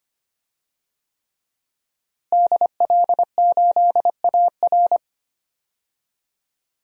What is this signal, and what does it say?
Morse code, heard as a keyed tone, about 25 wpm, 700 Hz. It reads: DL8AR
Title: DXCC-100 Call Signs - 5 Characters - Repeated 3x - 25wpm